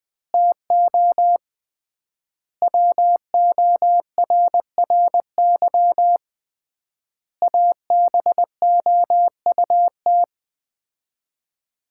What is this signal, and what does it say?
Morse code, heard as a keyed tone, about 20 wpm, 700 Hz. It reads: TO WORRY ABOUT
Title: 3x Word-Spacing - Sets of 5 Words - 20wpm